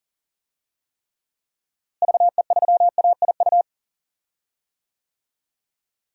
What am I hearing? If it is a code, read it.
Morse code, heard as a keyed tone, about 40 wpm, 700 Hz. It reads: VE3AIU
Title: DXCC-100 Call Signs - 6 Characters - Repeated 2x - 40wpm